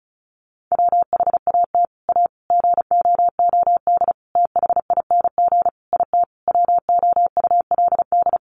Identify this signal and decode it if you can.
Morse code, heard as a keyed tone, about 34 wpm, 700 Hz. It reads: WHAT A GOOD THING IT WOULD
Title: Sentences from Top 200 Words 35wpm